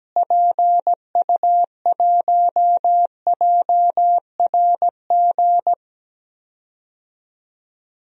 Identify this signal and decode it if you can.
Morse code, heard as a keyed tone, about 17 wpm, 700 Hz. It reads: PU1JRG
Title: DXCC-100 Call Signs - 6 Characters - Repeated 2x - 17wpm